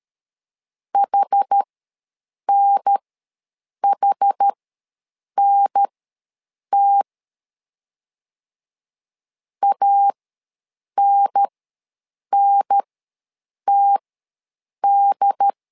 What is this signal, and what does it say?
Morse code, heard as a keyed tone, about 13 wpm, spaced as for 7 wpm, 800 Hz. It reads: HNHNT ANNTD
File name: LEZIONI / 2 UHDN / 2 UHDN ws 12wpm 1708